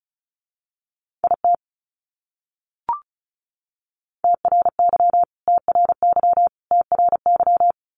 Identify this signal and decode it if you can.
Morse code, heard as a keyed tone, about 34 wpm, 700 Hz. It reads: IT TRY TRY TRY
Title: Top 200 Words Repeated 35wpm